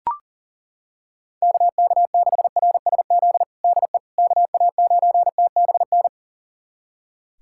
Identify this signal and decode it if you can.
Morse code, heard as a keyed tone, about 37 wpm, 700 Hz. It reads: KK6RSZ DE KA9TBN
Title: USA Callsign DE Callsign 40wpm